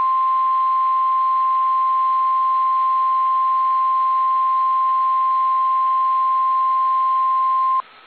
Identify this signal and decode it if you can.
Emergency Alert System (SAME) tones (the 1050 Hz attention tone). It is heard from 0.4 to 7.8 s.